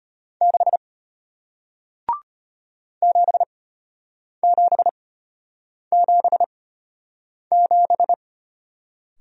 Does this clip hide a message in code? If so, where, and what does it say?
Yes: Morse code from 0.4 to 8.1 s, heard as a keyed tone, about 33 wpm, spaced as for 8 wpm, 700 Hz. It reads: 6 7777